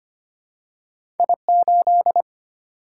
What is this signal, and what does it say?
Morse code, heard as a keyed tone, about 25 wpm, 700 Hz. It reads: I8
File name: Two Letter-Number Combinations - 25wpm